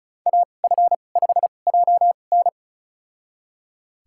Morse code, heard as a keyed tone, about 36 wpm, spaced as for 25 wpm, 700 Hz. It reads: AF5JN
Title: USA Call Signs - 5 Characters - 2x Character Spacing - 35wpm